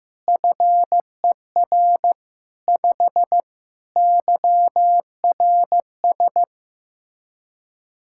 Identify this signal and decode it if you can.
Morse code, heard as a keyed tone, about 15 wpm, 700 Hz. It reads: FER 5 YRS
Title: Practice QSO - Exchange 3 - More Info - 15wpm